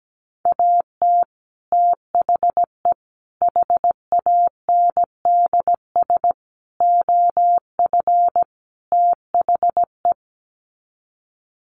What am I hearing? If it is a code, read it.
Morse code, heard as a keyed tone, about 17 wpm, 700 Hz. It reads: AT THE HANDS OF THE
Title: Sets of 5 Words Encore2 17wpm